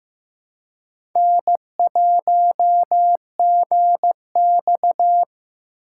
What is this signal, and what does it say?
Morse code, heard as a keyed tone, about 15 wpm, 700 Hz. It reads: N1GX
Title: USA Call Signs - 4 Characters - 4x Character Spacing - 15wpm